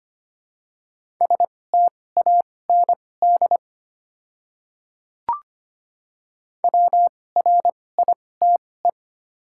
Morse code, heard as a keyed tone, about 25 wpm, spaced as for 18 wpm, 700 Hz. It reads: STAND WRITE
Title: 2x Character Spacing - Top 300 Words - 25wpm